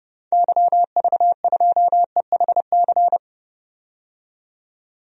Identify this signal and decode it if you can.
Morse code, heard as a keyed tone, about 30 wpm, 700 Hz. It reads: YV2EHC
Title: DXCC-100 Call Signs - 6 Characters - 30wpm